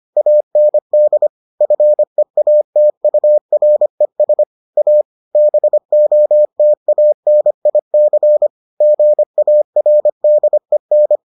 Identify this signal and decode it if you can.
Morse code, heard as a keyed tone, about 25 wpm, 600 Hz. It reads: AND FEATURES A BOTANIC GARDEN